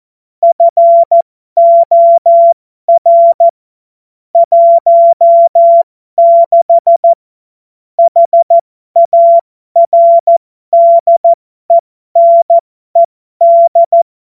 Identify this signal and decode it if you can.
Morse code, heard as a keyed tone, about 14 wpm, spaced as for 12 wpm, 680 Hz. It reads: FOR 16 HARDENED
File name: Morsekurs / Witze